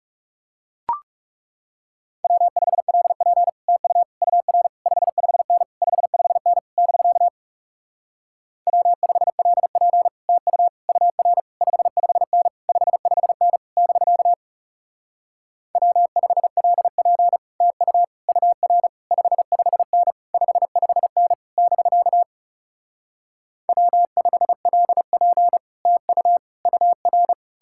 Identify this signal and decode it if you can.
Morse code, heard as a keyed tone, about 38 wpm, 700 Hz. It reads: W5LP TU UR 55N 55N <BK> W5LP TU UR 55N 55N <BK> W5LP TU UR 55N 55N <BK> W5LP TU UR
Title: Speed-Racing Common QSO Phrases Encore - 30wpm